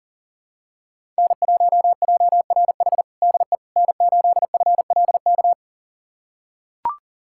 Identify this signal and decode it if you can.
Morse code, heard as a keyed tone, about 40 wpm, 700 Hz. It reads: N1JRH DE N8FLK
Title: USA Callsign DE Callsign - 5 Characters - 40wpm